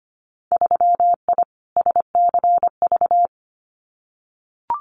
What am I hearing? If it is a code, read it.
Morse code, heard as a keyed tone, about 25 wpm, 700 Hz. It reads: 3I SCV
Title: Winter Field Day Exchanges Encore 25wpm